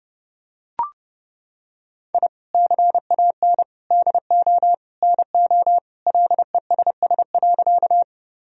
Morse code, heard as a keyed tone, about 32 wpm, 700 Hz. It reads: I CAN DO NO LESS.
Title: Sentences from Top 1000 Words - Review Part 2 - 30wpm